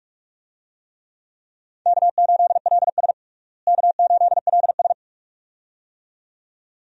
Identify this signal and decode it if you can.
Morse code, heard as a keyed tone, about 45 wpm, 700 Hz. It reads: K8LS K8LS
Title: USA Call Signs - 4 Characters - Repeated 2x - 45wpm